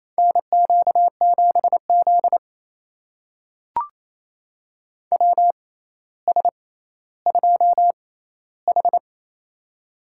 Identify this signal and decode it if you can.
Morse code, heard as a keyed tone, about 28 wpm, spaced as for 14 wpm, 700 Hz. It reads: NQ7Z WS2H
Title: USA Call Signs - 6x Character Spacing - 28wpm